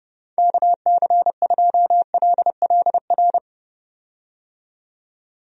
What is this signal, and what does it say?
Morse code, heard as a keyed tone, about 30 wpm, 700 Hz. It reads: KC2LLR
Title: USA Call Signs - 6 Characters - 30wpm